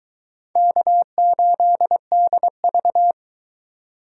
Morse code, heard as a keyed tone, about 22 wpm, 700 Hz. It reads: K8DV